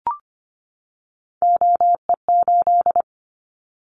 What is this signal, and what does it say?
Morse code, heard as a keyed tone, about 25 wpm, 700 Hz. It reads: OE8
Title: Callsign Prefixes for Top 100 DXCC - 25wpm